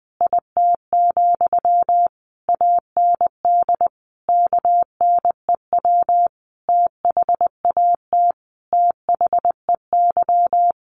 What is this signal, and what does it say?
Morse code, heard as a keyed tone, about 20 wpm, 700 Hz. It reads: IT, AND KNEW THAT THEY
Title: Sentences from Top 1200 Words 20wpm